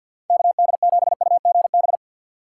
Morse code, heard as a keyed tone, about 47 wpm, 700 Hz. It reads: KD7UGB
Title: USA Call Signs - 6 Characters - 6x Character Spacing - 50wpm